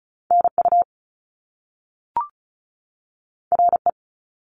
Morse code, heard as a keyed tone, about 36 wpm, 700 Hz. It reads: NU RE
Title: Top Two Letter Combinations in Words 35wpm